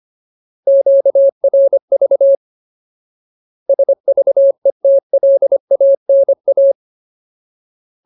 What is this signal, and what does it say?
Morse code, heard as a keyed tone, about 25 wpm, 550 Hz. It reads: QRV SVETLANA
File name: Ultimate Word List / 25WPM Ultimate